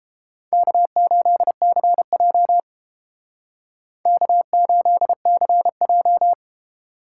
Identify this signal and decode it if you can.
Morse code, heard as a keyed tone, about 29 wpm, 700 Hz. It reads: K8CJ K8CJ